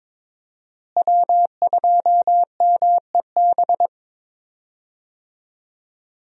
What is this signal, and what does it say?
Morse code, heard as a keyed tone, about 22 wpm, 700 Hz. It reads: W2MEB
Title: USA Call Signs - 5 Characters - Repeated 2x - 22wpm